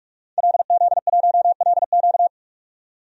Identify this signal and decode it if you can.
Morse code, heard as a keyed tone, about 42 wpm, 700 Hz. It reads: LZ1LQ
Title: DXCC-100 Call Signs - 5 Characters - 45wpm